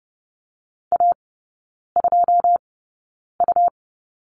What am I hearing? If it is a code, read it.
Morse code, heard as a keyed tone, about 30 wpm, spaced as for 9 wpm, 700 Hz. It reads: A2U
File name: Lesson 17e - Letter u - Three Characters 30wpm